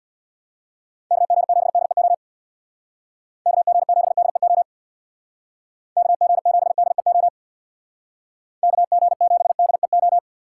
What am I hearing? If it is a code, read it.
Morse code, heard as a keyed tone, about 60 wpm, 700 Hz. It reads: KG7DEO KG7DEO KG7DEO KG7DEO
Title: Speed Racing - USA 2x3 Call Signs 50wpm